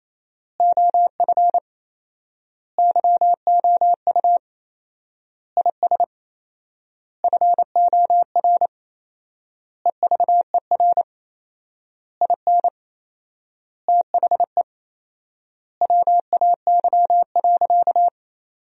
Morse code, heard as a keyed tone, about 28 wpm, 700 Hz. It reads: OF YOU IS FOR EVER IN THE WAY.
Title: 4x Word-Spacing - Sentences from Top 400 Words - 28wpm